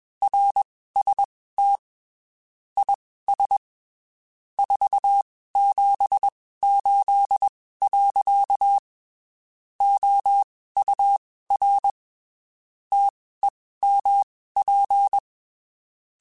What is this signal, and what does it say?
Morse code, heard as a keyed tone, about 21 wpm, spaced as for 15 wpm, 800 Hz. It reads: RST IS 478. OUR TEMP
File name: (spaced) / QSOs Spaced-mp3s / WD0FBT @20 sp